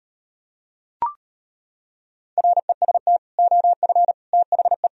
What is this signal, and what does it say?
Morse code, heard as a keyed tone, about 39 wpm, 700 Hz. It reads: REST OF THE